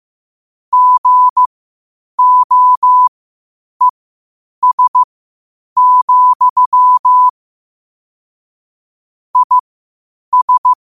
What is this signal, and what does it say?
Morse code, heard as a keyed tone, about 15 wpm, spaced as for 8 wpm, 1000 Hz. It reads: GOES, IS
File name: qotdfiles / 8wpm